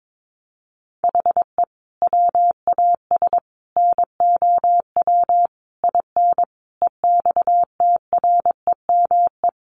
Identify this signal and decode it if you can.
Morse code, heard as a keyed tone, about 22 wpm, 700 Hz. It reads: HE WAS NOW IN EXTREME
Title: Sentences from Top 1300 Words 22wpm